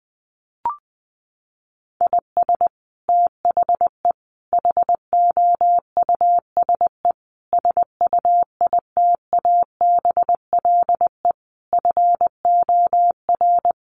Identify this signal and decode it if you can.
Morse code, heard as a keyed tone, about 20 wpm, 700 Hz. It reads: IS THE HOUSE SUITABLE FOR
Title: Sentences from Top 1400 Words 20wpm